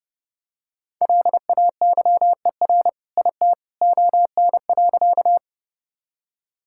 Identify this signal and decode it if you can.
Morse code, heard as a keyed tone, about 30 wpm, 700 Hz. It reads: LAYER IT ON.
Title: Sentences from Top 1600 Words 30wpm